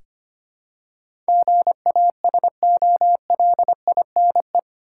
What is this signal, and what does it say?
Morse code, heard as a keyed tone, about 25 wpm, 700 Hz. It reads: GASOLINE
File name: practice / 25wpm / Easy Words with Hints - 25wpm